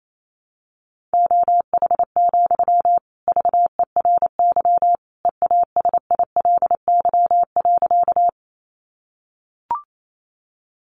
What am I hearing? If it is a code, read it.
Morse code, heard as a keyed tone, about 28 wpm, 700 Hz. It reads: OH, VERY EASILY.
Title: Sentences from Top 900 Words 28wpm